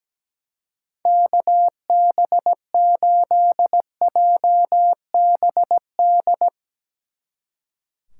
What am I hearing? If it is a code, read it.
Morse code, heard as a keyed tone, about 18 wpm, 700 Hz. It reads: KB8JBD
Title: Speed Racing - USA 2x3 Call Signs 17wpm